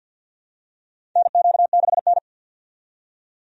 Morse code, heard as a keyed tone, about 51 wpm, 700 Hz. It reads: NQ6N
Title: USA Call Signs - 4 Characters - 50wpm